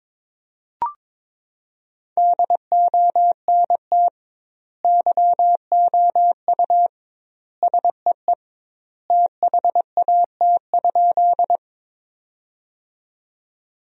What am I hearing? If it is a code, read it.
Morse code, heard as a keyed tone, about 22 wpm, 700 Hz. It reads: DONT YOU SEE THAT?
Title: 2x Word-Spacing - Sentences from Top 300 Words - 22wpm